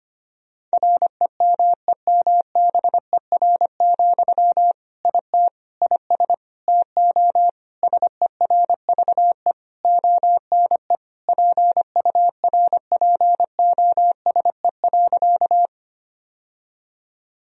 Morse code, heard as a keyed tone, about 25 wpm, 700 Hz. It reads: REMEMBER, IT IS TO SERVE ONE PURPOSE.
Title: Sentences from Top 1000 Words - Review Part 2 - 25wpm